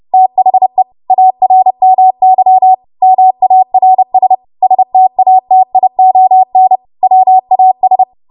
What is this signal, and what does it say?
Morse code, heard as a keyed tone, about 30 wpm, 750 Hz. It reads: THE ARMY MARS STATION WAS
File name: Archive / 30 WPM / 30WPM